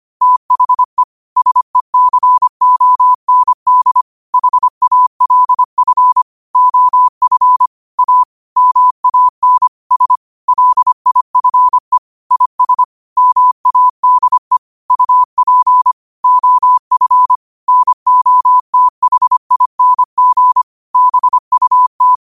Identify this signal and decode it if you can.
Morse code, heard as a keyed tone, about 25 wpm, 1000 Hz. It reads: THE SECOND HALF OF A MAN S LIFE IS MADE UP OF NOTHING BUT
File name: qotdfiles / 25wpm